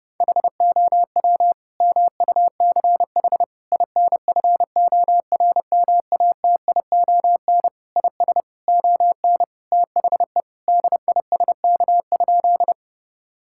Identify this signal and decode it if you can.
Morse code, heard as a keyed tone, about 30 wpm, 700 Hz. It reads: HOW MUCH INFORMATION IS ON THE DISK?
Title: Sentences from Top 1000 Words - Review Part 6 - 30wpm